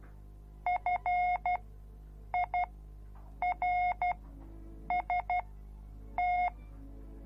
Morse code, heard as a keyed tone, about 12 wpm, spaced as for 7 wpm, 720 Hz. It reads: FIRST